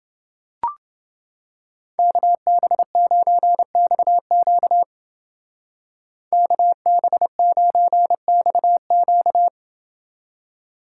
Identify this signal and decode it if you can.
Morse code, heard as a keyed tone, about 28 wpm, 700 Hz. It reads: KB9XQ KB9XQ